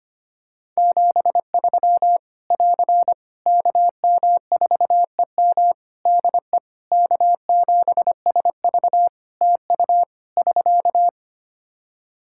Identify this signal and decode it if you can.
Morse code, heard as a keyed tone, about 25 wpm, 700 Hz. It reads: 73 <AR> KM4EM DE K7SV TU <SK>
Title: Practice QSO - Exchange 4 - Ending - 25wpm